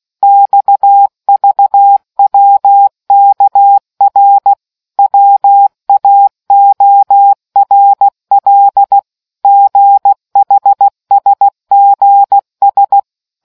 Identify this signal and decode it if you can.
Morse code, heard as a keyed tone, about 16 wpm, 800 Hz. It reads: XVWKR WAORL GHSGS